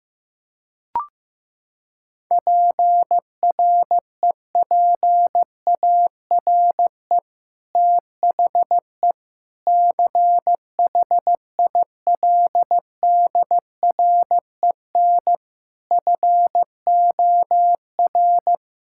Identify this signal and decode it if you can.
Morse code, heard as a keyed tone, about 15 wpm, 700 Hz. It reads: PREPARE THE CHILDREN FOR
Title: Sentences from Top 1000 Words - Review Part 6 - 15wpm